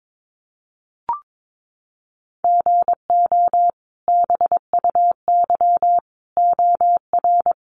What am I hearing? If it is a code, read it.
Morse code, heard as a keyed tone, about 22 wpm, 700 Hz. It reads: GO BUY OR
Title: Sentences from Top 1500 Words 22wpm